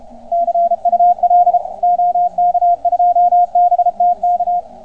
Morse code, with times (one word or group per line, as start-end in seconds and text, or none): 0.3-1.6 CUL
1.8-4.6 OK2BTK